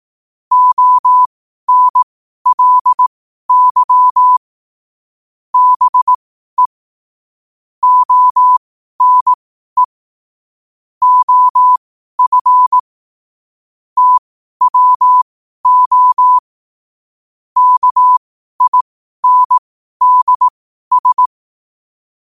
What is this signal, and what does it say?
Morse code, heard as a keyed tone, about 18 wpm, spaced as for 12 wpm, 1000 Hz. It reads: ONLY BE ONE OF TWO KINDS
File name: qotdfiles / 12wpm